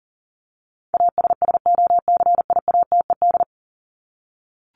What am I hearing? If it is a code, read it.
Morse code, heard as a keyed tone, about 40 wpm, 700 Hz. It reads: ASSOCIATED
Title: ED Suffix - Top 100 Words with Base Words 40wpm